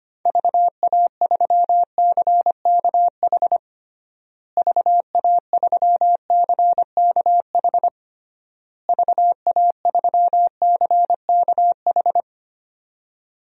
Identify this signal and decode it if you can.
Morse code, heard as a keyed tone, about 25 wpm, 700 Hz. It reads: VA3CKH VA3CKH VA3CKH
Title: DXCC-100 Call Signs - 6 Characters - Repeated 3x - 25wpm